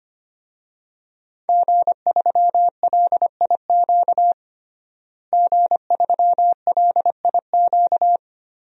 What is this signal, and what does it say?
Morse code, heard as a keyed tone, about 25 wpm, 700 Hz. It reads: G3LIQ G3LIQ
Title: DXCC-100 Call Signs - 5 Characters - Repeated 3x - 25wpm